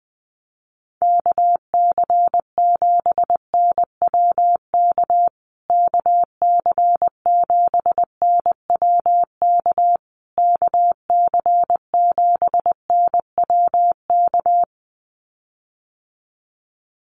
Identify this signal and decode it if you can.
Morse code, heard as a keyed tone, about 20 wpm, 700 Hz. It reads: KC7NWK KC7NWK KC7NWK
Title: USA Call Signs Repeated 3x 20wpm